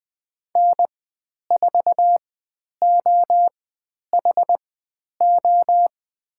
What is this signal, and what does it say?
Morse code, heard as a keyed tone, about 20 wpm, spaced as for 10 wpm, 700 Hz. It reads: N4OHO
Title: 109 Lesson 18g - Letter m - Call Signs 20wpm